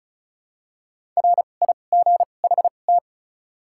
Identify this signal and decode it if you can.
Morse code, heard as a keyed tone, about 34 wpm, spaced as for 25 wpm, 700 Hz. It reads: RIGHT
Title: 2x Character Spacing - Top 200 Words - 35wpm